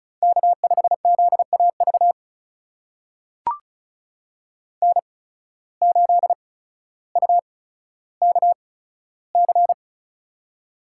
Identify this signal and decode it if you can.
Morse code, heard as a keyed tone, about 36 wpm, spaced as for 15 wpm, 700 Hz. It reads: K5ZAV N8UKC
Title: USA Call Signs - 8x Character Spacing - 35wpm